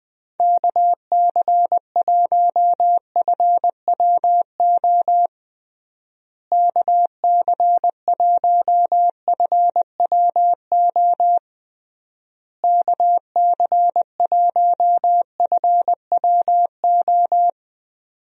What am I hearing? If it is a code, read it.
Morse code, heard as a keyed tone, about 20 wpm, 700 Hz. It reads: KC1FWO KC1FWO KC1FWO
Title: USA Call Signs - 6 Characters - Repeated 3x - 20wpm